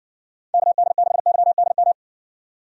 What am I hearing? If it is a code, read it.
Morse code, heard as a keyed tone, about 58 wpm, 700 Hz. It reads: KD6YDM